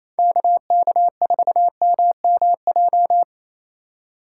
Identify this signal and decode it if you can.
Morse code, heard as a keyed tone, about 28 wpm, 700 Hz. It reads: KK4MMJ
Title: USA Call Signs - 6 Characters - 4x Character Spacing - 28wpm